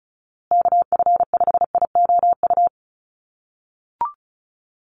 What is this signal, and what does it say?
Morse code, heard as a keyed tone, about 34 wpm, 700 Hz. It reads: KF5IOU
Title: USA Call Signs Repeated 2x 35wpm